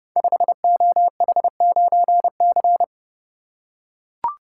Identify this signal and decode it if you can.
Morse code, heard as a keyed tone, about 30 wpm, 700 Hz. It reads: 5OH9C
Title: Code Groups - 5 Letter:Numbers - 30wpm